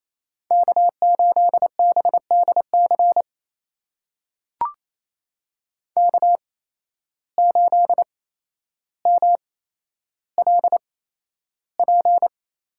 Morse code, heard as a keyed tone, about 28 wpm, spaced as for 12 wpm, 700 Hz. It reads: K8BDC K8MLP
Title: USA Call Signs - 5 Characters - 8x Character Spacing - 28wpm